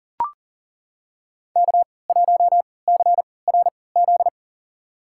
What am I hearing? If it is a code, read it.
Morse code, heard as a keyed tone, about 40 wpm, spaced as for 23 wpm, 700 Hz. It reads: K1CRZ
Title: USA Call Signs - 5 Characters - 3x Character Spacing - 40wpm